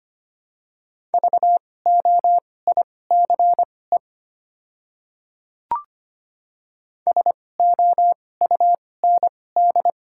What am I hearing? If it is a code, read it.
Morse code, heard as a keyed tone, about 25 wpm, spaced as for 18 wpm, 700 Hz. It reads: VOICE SOUND
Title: 2x Character Spacing - Top 500 Words - 25wpm